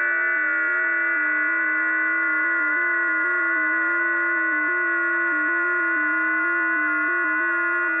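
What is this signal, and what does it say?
FT8.
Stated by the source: FT8 signal received on a WebSDR.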